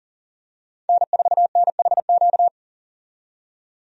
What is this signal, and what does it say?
Morse code, heard as a keyed tone, about 40 wpm, 700 Hz. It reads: N4NHQ
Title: USA Call Signs - 5 Characters - 40wpm